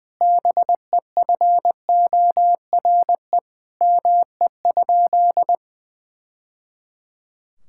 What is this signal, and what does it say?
Morse code, heard as a keyed tone, about 20 wpm, 700 Hz. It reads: BEFORE ME?
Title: Sentences from Top 1000 Words - Review Part 5 - 20wpm